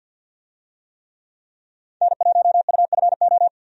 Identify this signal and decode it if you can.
Morse code, heard as a keyed tone, about 47 wpm, 700 Hz. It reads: N1UFO
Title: USA Call Signs - 5 Characters - 50wpm